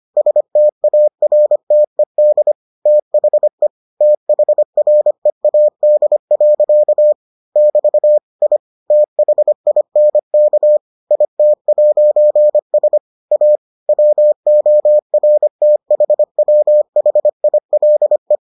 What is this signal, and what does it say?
Morse code, heard as a keyed tone, about 25 wpm, 600 Hz. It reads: STARTED THE THREAD. <BT> I THINK IT'S A WORTHWHILE